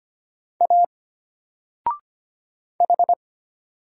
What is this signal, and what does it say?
Morse code, heard as a keyed tone, about 25 wpm, 700 Hz. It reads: A H
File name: Single-Letter-Number-25wpm